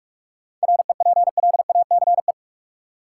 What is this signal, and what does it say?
Morse code, heard as a keyed tone, about 46 wpm, 700 Hz. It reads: REPLACE
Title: Sentences from Top 1400 Words - New Word Drill - 45wpm